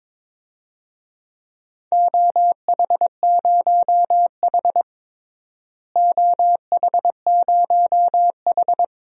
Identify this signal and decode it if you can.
Morse code, heard as a keyed tone, about 22 wpm, 700 Hz. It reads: OH0H OH0H
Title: DXCC-100 Call Signs - 4 Characters - Repeated 2x - 22wpm